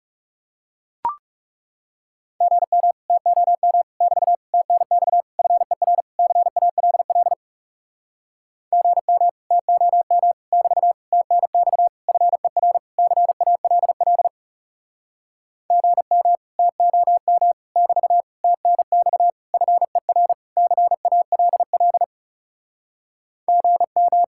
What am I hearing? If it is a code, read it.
Morse code, heard as a keyed tone, about 40 wpm, 700 Hz. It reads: GM TOM <BT> TNX FER CALL GM TOM <BT> TNX FER CALL GM TOM <BT> TNX FER CALL GM